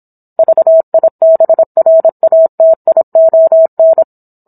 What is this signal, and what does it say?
Morse code, heard as a keyed tone, about 27 wpm, 650 Hz. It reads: VIBRATION
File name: Book One / 17-Chapter 17 - 26 WPM